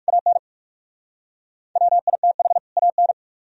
Morse code, heard as a keyed tone, about 53 wpm, 700 Hz. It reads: AN WITH AN
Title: Speed-Racing - Sets of 2 Words 45wpm